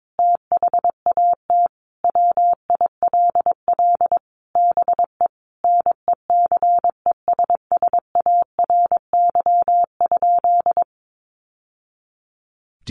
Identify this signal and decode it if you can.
Morse code, heard as a keyed tone, about 22 wpm, 700 Hz. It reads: THAT WILL BE NECESSARY?
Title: Sentences from Top 800 Words 22wpm